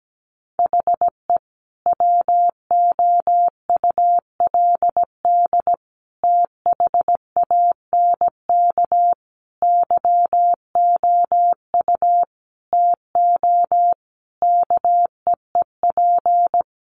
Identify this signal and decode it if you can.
Morse code, heard as a keyed tone, about 17 wpm, 700 Hz. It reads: HE WOULD THANK YOU TO KEEP
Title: Sentences from Top 1100 Words 17wpm